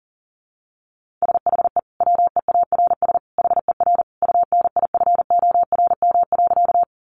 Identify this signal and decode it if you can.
Morse code, heard as a keyed tone, about 40 wpm, 700 Hz. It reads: SHE WEARS HER UNIFORM.
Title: Easy Phrases 40wpm